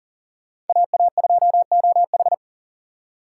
Morse code, heard as a keyed tone, about 51 wpm, 700 Hz. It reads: AA2OH